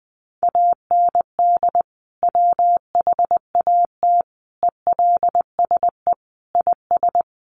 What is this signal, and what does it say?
Morse code, heard as a keyed tone, about 20 wpm, 700 Hz. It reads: AND WHAT ELSE IS
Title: Sentences from Top 1500 Words 20wpm